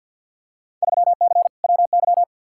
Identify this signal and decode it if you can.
Morse code, heard as a keyed tone, about 46 wpm, 700 Hz. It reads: 3C WY
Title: Field Day Exchanges Encore - 50wpm